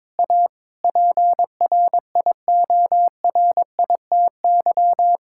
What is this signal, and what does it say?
Morse code, heard as a keyed tone, about 22 wpm, 700 Hz. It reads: A PRIORITY
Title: Sentences from Top 1600 Words 22wpm